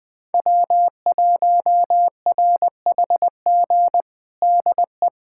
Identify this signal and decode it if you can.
Morse code, heard as a keyed tone, about 20 wpm, 700 Hz. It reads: W1RHG DE
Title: USA Callsign DE Callsign - 5 Characters - 20wpm